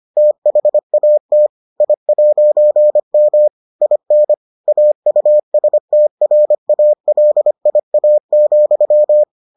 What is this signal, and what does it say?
Morse code, heard as a keyed tone, about 25 wpm, 600 Hz. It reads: THAT I'M IN AUSTRALIA,